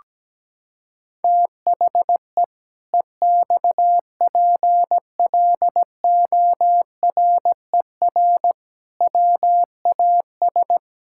Morse code, heard as a keyed tone, about 17 wpm, 700 Hz. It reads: THE EXPLORER WAS
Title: Pangram Challenge v2 - 17wpm